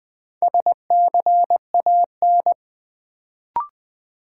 Morse code, heard as a keyed tone, about 20 wpm, 700 Hz. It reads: SCAN
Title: Build A Word - 2 Letters At A Time - 4-Letter Words - 20wpm